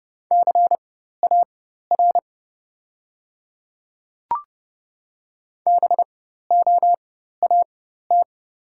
Morse code, heard as a keyed tone, about 30 wpm, spaced as for 14 wpm, 700 Hz. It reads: CAR BOAT
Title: 4x Character Spacing - Top 500 Words - 30wpm